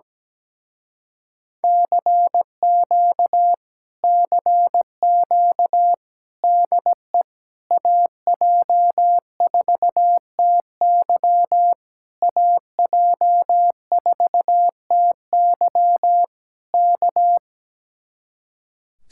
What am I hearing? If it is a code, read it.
Morse code, heard as a keyed tone, about 17 wpm, 700 Hz. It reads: CQ CQ DE AJ4TY AJ4TY K
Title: Speed-Racing Common QSO Phrases - 17wpm